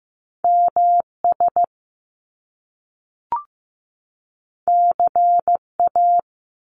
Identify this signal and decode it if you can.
Morse code, heard as a keyed tone, about 15 wpm, 700 Hz. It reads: MS CA
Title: US States 15wpm